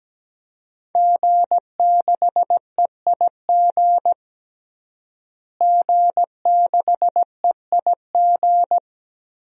Morse code, heard as a keyed tone, about 17 wpm, 700 Hz. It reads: G6EIG G6EIG
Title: DXCC-100 Call Signs - 5 Characters - Repeated 3x - 17wpm